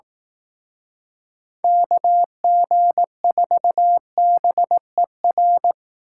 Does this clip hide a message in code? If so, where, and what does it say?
Yes: Morse code from 1.6 to 5.7 s, heard as a keyed tone, about 19 wpm, 700 Hz. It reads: KG4BER